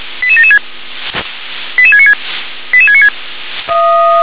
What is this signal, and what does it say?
Five-tone selective call (ZVEI), heard as quick runs of five tones, tones of about 70 ms. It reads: 88776 88696 88696
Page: /357/SDR/ZVEI/88771/2022.11.01/